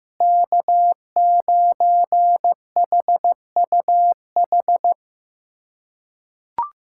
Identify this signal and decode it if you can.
Morse code, heard as a keyed tone, about 15 wpm, 700 Hz. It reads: K9HUH
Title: USA Callsign DE Callsign - 5 Characters - 15wpm